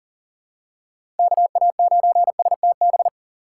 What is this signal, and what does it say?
Morse code, heard as a keyed tone, about 38 wpm, 700 Hz. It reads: KA9STB
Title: USA Call Signs - 4x Character Spacing - 40wpm